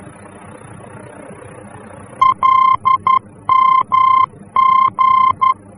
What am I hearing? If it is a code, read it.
Morse code, heard as a keyed tone, about 11 wpm, 1050 Hz. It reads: LMG
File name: france / mx / LMG-F-VOR